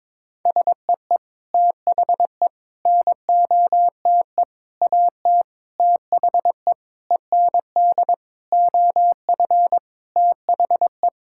Morse code, heard as a keyed tone, about 22 wpm, 700 Hz. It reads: SEE THE NOTE AT THE END OF THE
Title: Sentences from Top 1000 Words - Review Part 5 - 22wpm